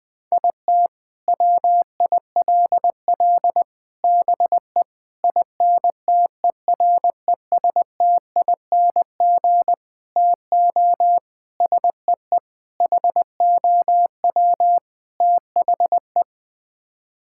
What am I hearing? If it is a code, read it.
Morse code, heard as a keyed tone, about 20 wpm, 700 Hz. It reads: IT WILL BE INTERESTING TO SEE HOW THE
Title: Sets of 8 Words - 20wpm